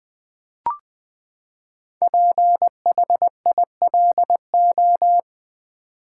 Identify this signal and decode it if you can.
Morse code, heard as a keyed tone, about 20 wpm, 700 Hz. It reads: PHILO
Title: Build A Word - 5 Letters At A Time - 10-Letter Words - 20wpm